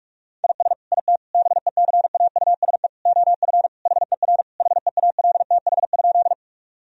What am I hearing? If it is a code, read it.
Morse code, heard as a keyed tone, about 41 wpm, 700 Hz. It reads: IS IT BECAUSE OF HER HEALTH?
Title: Sentences from Top 1000 Words - Review Part 3 - 45wpm